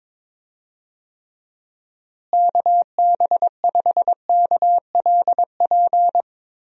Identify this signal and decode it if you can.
Morse code, heard as a keyed tone, about 21 wpm, 700 Hz. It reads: KB5KLP
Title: USA Call Signs - 6 Characters - Repeated 2x - 22wpm